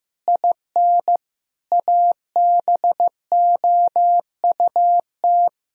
Morse code, heard as a keyed tone, about 16 wpm, 700 Hz. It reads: IN ABOUT
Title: Sentences from Top 1000 Words - Review Part 1 - 15wpm